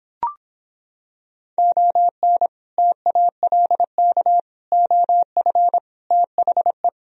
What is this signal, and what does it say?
Morse code, heard as a keyed tone, about 26 wpm, 700 Hz. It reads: ON TALK OF THE